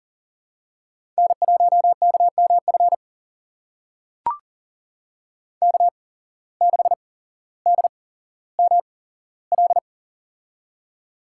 Morse code, heard as a keyed tone, about 38 wpm, spaced as for 17 wpm, 700 Hz. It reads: N1KMF K6DML
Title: USA Call Signs - 8x Character Spacing - 40wpm